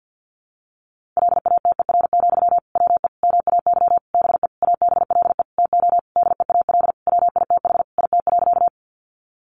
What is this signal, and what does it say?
Morse code, heard as a keyed tone, about 50 wpm, 700 Hz. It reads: LATER, WE MAY BE ABLE TO DEAL WITH IT.
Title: Sentences from Top 800 Words 50wpm